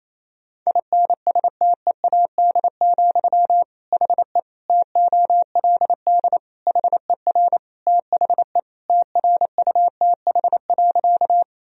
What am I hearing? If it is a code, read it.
Morse code, heard as a keyed tone, about 28 wpm, 700 Hz. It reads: INSTEAD, HE TOLD HER THE TRUTH.
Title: Sentences from Top 1000 Words - Review Part 4 - 28wpm